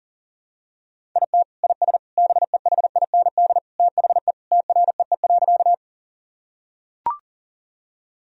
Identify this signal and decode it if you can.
Morse code, heard as a keyed tone, about 40 wpm, 700 Hz. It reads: IT IS BEHIND THE TREE.
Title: Sentences from Top 1000 Words - Review Part 1 - 40wpm